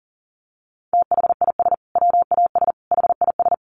Morse code, heard as a keyed tone, about 40 wpm, 700 Hz. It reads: THIS WAS HIS
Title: Sentences from Top 1200 Words 40wpm